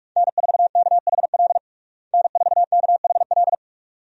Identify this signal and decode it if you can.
Morse code, heard as a keyed tone, about 46 wpm, 700 Hz. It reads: N4KHL N4KHL
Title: USA Call Signs - 5 Characters - Repeated 2x - 45wpm